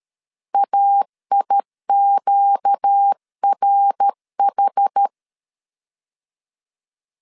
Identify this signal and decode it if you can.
Morse code, heard as a keyed tone, about 13 wpm, 800 Hz. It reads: AIQRH
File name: LEZIONI / 6 PKX0 / 6 PK 12wpm 2006